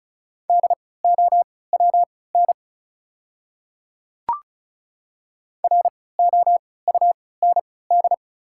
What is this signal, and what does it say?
Morse code, heard as a keyed tone, about 34 wpm, spaced as for 20 wpm, 700 Hz. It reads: DOWN ROUND
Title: 3x Character Spacing - Top 200 Words - 35wpm